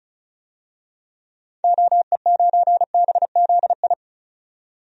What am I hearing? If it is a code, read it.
Morse code, heard as a keyed tone, about 35 wpm, 700 Hz. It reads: OE9BZI
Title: DXCC-100 Call Signs - 6 Characters - 35wpm